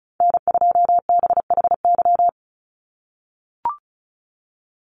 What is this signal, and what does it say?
Morse code, heard as a keyed tone, about 34 wpm, 700 Hz. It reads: N2BHY
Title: USA Call Signs 35wpm